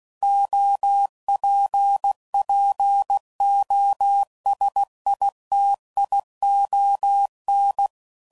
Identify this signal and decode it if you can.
Morse code, heard as a keyed tone, about 16 wpm, 800 Hz. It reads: OPPOSITION
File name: CW Code Groups @15WPM / Technical Subjects / XC